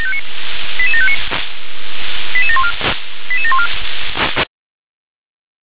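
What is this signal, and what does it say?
Five-tone selective call (ZVEI), heard as quick runs of five tones, tones of about 70 ms. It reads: 88759 88714 88714
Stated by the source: Five-tone ZVEI selective call, SDR-received, address 88758.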